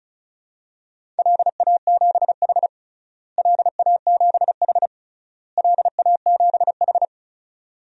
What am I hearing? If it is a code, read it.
Morse code, heard as a keyed tone, about 36 wpm, 700 Hz. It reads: LA7H LA7H LA7H
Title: DXCC-100 Call Signs - 4 Characters - Repeated 3x - 35wpm